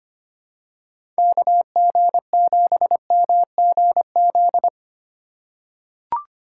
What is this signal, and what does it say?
Morse code, heard as a keyed tone, about 25 wpm, 700 Hz. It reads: KG7MGZ
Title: USA Call Signs - 2x Character Spacing - 25wpm